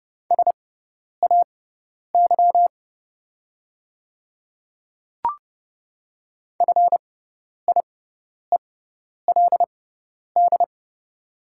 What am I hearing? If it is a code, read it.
Morse code, heard as a keyed tone, about 30 wpm, spaced as for 10 wpm, 690 Hz. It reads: SAY FIELD
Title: 6x Character Spacing - Top 500 Words - 30wpm